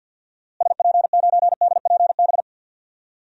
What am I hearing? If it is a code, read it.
Morse code, heard as a keyed tone, about 46 wpm, 700 Hz. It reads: SP9DPB
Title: DXCC-100 Call Signs - 6 Characters - Repeated 3x - 50wpm